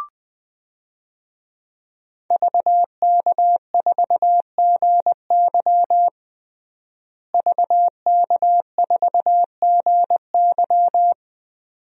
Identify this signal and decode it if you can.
Morse code, heard as a keyed tone, about 20 wpm, 700 Hz. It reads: VK4GY VK4GY
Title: DXCC-100 Call Signs - 5 Characters - Repeated 2x - 20wpm